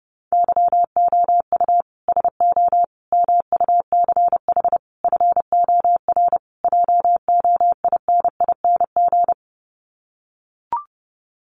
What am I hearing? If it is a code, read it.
Morse code, heard as a keyed tone, about 30 wpm, 700 Hz. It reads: YOU SO MUCH FOR JOINING
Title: Sets of 5 Words Encore2 30wpm